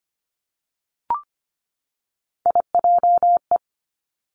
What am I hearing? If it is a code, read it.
Morse code, heard as a keyed tone, about 25 wpm, 700 Hz. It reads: IJE
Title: Three Challenging Letter-Number Combinations 25wpm